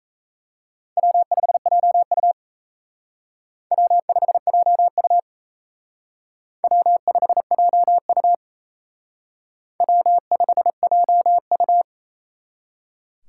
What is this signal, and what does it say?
Morse code, heard as a keyed tone, about 35 wpm, 700 Hz. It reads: W5JU W5JU W5JU W5JU
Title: Speed Racing - USA 1x2 Call Signs 28wpm